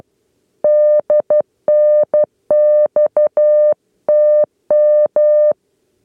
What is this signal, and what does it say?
Morse code, heard as a keyed tone, about 12 wpm, 600 Hz. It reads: DNX TM
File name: 8 lezione cw 2